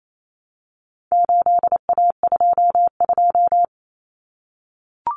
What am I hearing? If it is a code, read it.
Morse code, heard as a keyed tone, about 28 wpm, 700 Hz. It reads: 8A22
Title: Common Cut Numbers - 4-Digit 28wpm